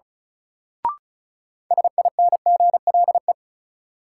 Morse code, heard as a keyed tone, about 34 wpm, 700 Hz. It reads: SINGLE
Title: Sentences from Top 700 Words - New Word Drill - 35wpm